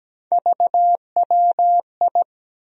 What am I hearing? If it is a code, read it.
Morse code, heard as a keyed tone, about 17 wpm, 700 Hz. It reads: VWI
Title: USA Call Sign Postfixes - 3 Letters 17wpm